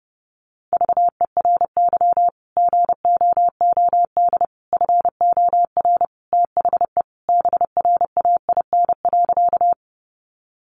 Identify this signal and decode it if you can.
Morse code, heard as a keyed tone, about 30 wpm, 700 Hz. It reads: VERY GOOD FOR THE BRAIN.
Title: Sentences from Top 1100 Words 30wpm